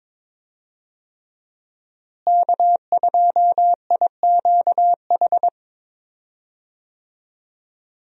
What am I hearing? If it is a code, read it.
Morse code, heard as a keyed tone, about 22 wpm, 700 Hz. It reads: K2IQH
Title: USA Call Signs - 5 Characters - Repeated 3x - 22wpm